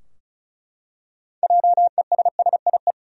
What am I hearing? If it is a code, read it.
Morse code, heard as a keyed tone, about 34 wpm, 700 Hz. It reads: JESSIE
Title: USA Common Names - 35wpm